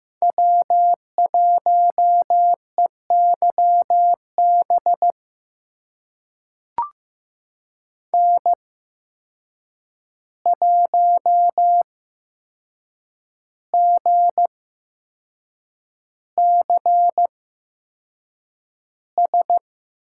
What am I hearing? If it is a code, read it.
Morse code, heard as a keyed tone, about 15 wpm, spaced as for 6 wpm, 700 Hz. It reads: W1EYB N1GCS
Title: USA Call Signs - 5 Characters - 8x Character Spacing - 15wpm